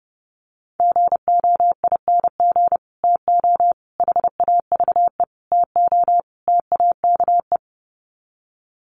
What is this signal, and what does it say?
Morse code, heard as a keyed tone, about 30 wpm, 700 Hz. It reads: GOING TO HAVE TO TAKE
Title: Sets of 5 Words Encore 30wpm